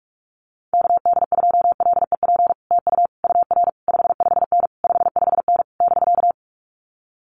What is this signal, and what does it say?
Morse code, heard as a keyed tone, about 44 wpm, 700 Hz. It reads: KD2LEP TU UR 55N 55N <BK>
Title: Common QSO Phrases Encore 45wpm